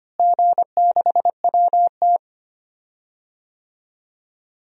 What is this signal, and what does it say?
Morse code, heard as a keyed tone, about 25 wpm, 700 Hz. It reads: G6WT
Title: DXCC-100 Call Signs - 4 Characters - Repeated 2x - 25wpm